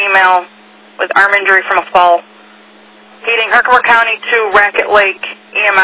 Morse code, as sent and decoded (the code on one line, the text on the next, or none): none